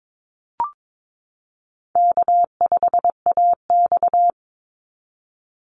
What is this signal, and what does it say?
Morse code, heard as a keyed tone, about 22 wpm, 700 Hz. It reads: K5AX
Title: Easy Call Signs - 22wpm